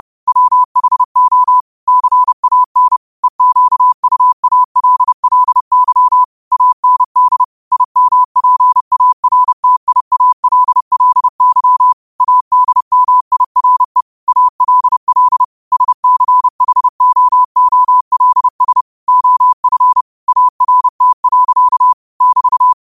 Morse code, heard as a keyed tone, about 30 wpm, 1000 Hz. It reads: WHO CAN EQUALLY AND IMPARTIALLY ADMIRE ALL SCHOOLS OF ART. <BT>